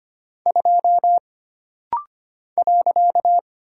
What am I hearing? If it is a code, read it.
Morse code, heard as a keyed tone, about 25 wpm, 700 Hz. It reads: 2 .
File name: Single Character (Letter:Number:Punctuation:Prosign) - Rapid-Fire - 25wpm